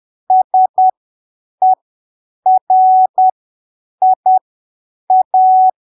Morse code, heard as a keyed tone, about 10 wpm, spaced as for 7 wpm, 750 Hz. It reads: SERIA
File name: arquivos / texto / texto 15 10ppm